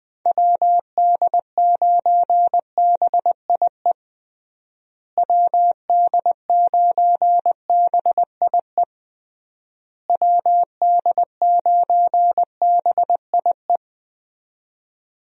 Morse code, heard as a keyed tone, about 21 wpm, 700 Hz. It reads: WD9BIE WD9BIE WD9BIE
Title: USA Call Signs - 6 Characters - Repeated 3x - 20wpm